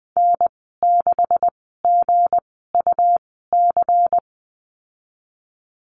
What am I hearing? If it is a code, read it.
Morse code, heard as a keyed tone, about 20 wpm, spaced as for 15 wpm, 700 Hz. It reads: N6GUC
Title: 193 Lesson 31g - Letter v - Call Signs 20wpm